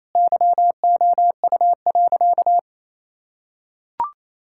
Morse code, heard as a keyed tone, about 28 wpm, 700 Hz. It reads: YOU.
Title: Sentences from Top 1600 Words 28wpm